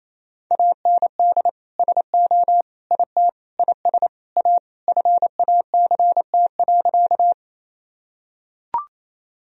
Morse code, heard as a keyed tone, about 28 wpm, 700 Hz. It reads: AND SO IT IS A FACT.
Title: Sentences from Top 1000 Words - Review Part 3 - 28wpm